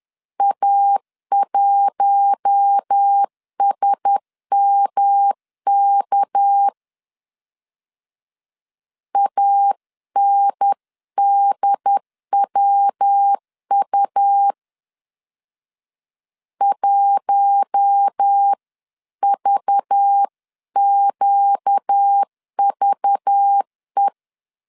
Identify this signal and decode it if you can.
Morse code, heard as a keyed tone, about 10 wpm, 800 Hz. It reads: A1SMK ANDWU 1VQVE